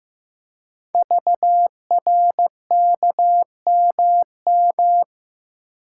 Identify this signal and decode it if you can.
Morse code, heard as a keyed tone, about 15 wpm, 700 Hz. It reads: VRKMM
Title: Code Groups - 5 Letters - 15wpm